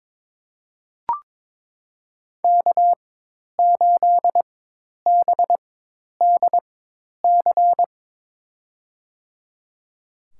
Morse code, heard as a keyed tone, about 22 wpm, spaced as for 10 wpm, 700 Hz. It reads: K8BDC
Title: USA Call Signs - 5 Characters - 4x Character Spacing - 22wpm